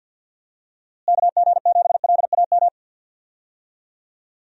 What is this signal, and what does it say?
Morse code, heard as a keyed tone, about 51 wpm, 700 Hz. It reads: KG7LAM
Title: USA Call Signs - 6 Characters - Repeated 3x - 50wpm